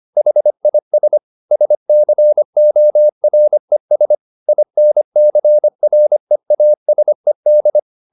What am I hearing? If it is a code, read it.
Morse code, heard as a keyed tone, about 25 wpm, 600 Hz. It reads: HIS SCORES INCREASED
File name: SS305 25